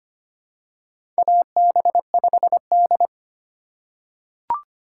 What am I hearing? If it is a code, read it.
Morse code, heard as a keyed tone, about 25 wpm, 700 Hz. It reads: AB5D
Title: USA Call Signs - 4 Characters - 4x Character Spacing - 25wpm